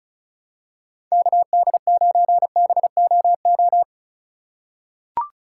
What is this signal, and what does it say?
Morse code, heard as a keyed tone, about 36 wpm, 700 Hz. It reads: KD9BOO
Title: USA Call Signs - 2x Character Spacing - 35wpm